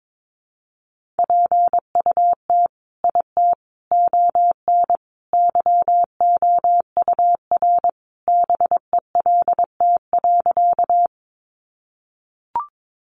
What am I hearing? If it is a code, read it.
Morse code, heard as a keyed tone, about 22 wpm, 700 Hz. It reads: PUT IT ON YOUR BELT.
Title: Sentences from Top 1400 Words 22wpm